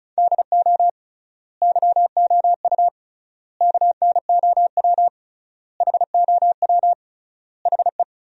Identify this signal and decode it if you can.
Morse code, heard as a keyed tone, about 34 wpm, 700 Hz. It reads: DO YOU KNOW HOW HE
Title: 3x Word-Spacing - Sentences from Top 300 Words - 35wpm